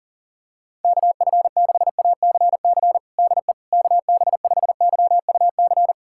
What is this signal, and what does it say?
Morse code, heard as a keyed tone, about 40 wpm, 700 Hz. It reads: KF6ACC DE KB5YUC
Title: USA Callsign DE Callsign 40wpm